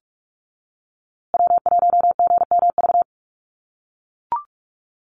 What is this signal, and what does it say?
Morse code, heard as a keyed tone, about 44 wpm, 700 Hz. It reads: W1GMV
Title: USA Call Signs Repeated 3x 45wpm